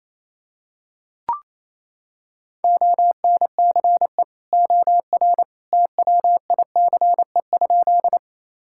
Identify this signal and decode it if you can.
Morse code, heard as a keyed tone, about 28 wpm, 700 Hz. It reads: ONCE OR TWICE?
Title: Sentences from Top 1600 Words 28wpm